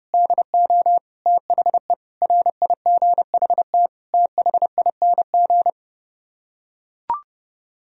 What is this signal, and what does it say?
Morse code, heard as a keyed tone, about 30 wpm, 700 Hz. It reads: DO THE RIGHT THING